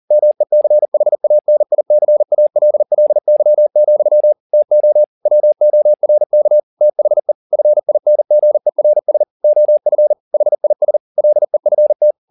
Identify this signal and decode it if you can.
Morse code, heard as a keyed tone, about 40 wpm, 600 Hz. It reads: MECHANICALLY, TO WORK THE FINGERS OF HIS LEFT